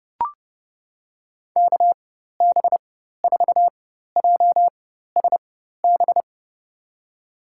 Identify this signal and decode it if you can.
Morse code, heard as a keyed tone, about 32 wpm, spaced as for 14 wpm, 700 Hz. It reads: KB4JSB
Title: USA Call Signs - 6 Characters - 4x Character Spacing - 30wpm